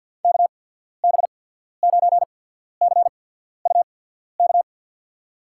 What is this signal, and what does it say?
Morse code, heard as a keyed tone, about 48 wpm, spaced as for 14 wpm, 700 Hz. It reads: KB9CUK
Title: USA Call Signs - 6 Characters - 8x Character Spacing - 50wpm